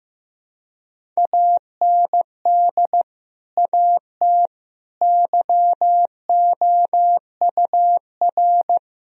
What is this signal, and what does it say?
Morse code, heard as a keyed tone, about 15 wpm, 700 Hz. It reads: AND AT YOUR
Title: Sentences from Top 1000 Words - Review Part 1 - 15wpm